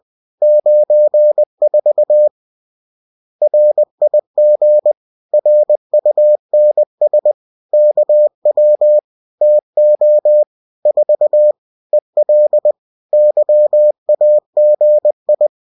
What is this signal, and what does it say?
Morse code, heard as a keyed tone, about 20 wpm, 600 Hz. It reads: 94 RIG RUNS KW TO 4 EL YAGI